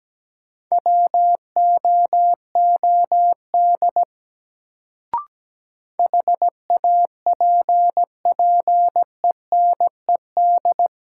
Morse code, heard as a keyed tone, about 17 wpm, 700 Hz. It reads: WOOD HAPPENED
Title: Sentences from Top 1000 Words - Review Part 2 - New Word Drill - 17wpm